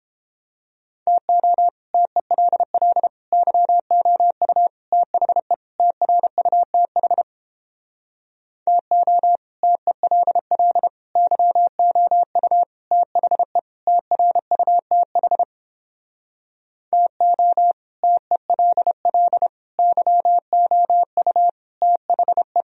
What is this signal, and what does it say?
Morse code, heard as a keyed tone, about 31 wpm, 700 Hz. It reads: TO TELL YOU THE TRUTH TO TELL YOU THE TRUTH TO TELL YOU THE